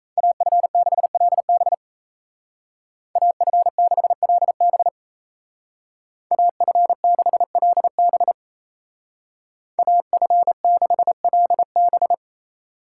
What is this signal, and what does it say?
Morse code, heard as a keyed tone, about 36 wpm, 700 Hz. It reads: AF6LB AF6LB AF6LB AF6LB